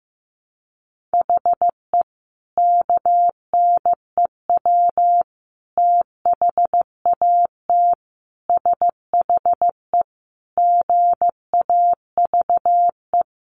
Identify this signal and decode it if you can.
Morse code, heard as a keyed tone, about 15 wpm, 700 Hz. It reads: HE KNEW THAT SHE GAVE
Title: Sentences from Top 1100 Words 15wpm